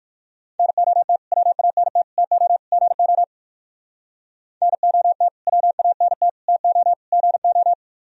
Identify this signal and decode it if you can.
Morse code, heard as a keyed tone, about 57 wpm, 700 Hz. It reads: NOT WANT TO GO NOT WANT TO GO